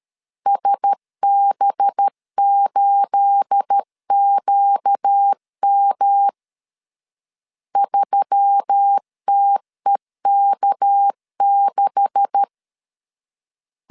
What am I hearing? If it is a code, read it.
Morse code, heard as a keyed tone, about 13 wpm, 800 Hz. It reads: SB8QM 3TEK6
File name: CW / DET / 1 9 1500C S B 12wpm 3819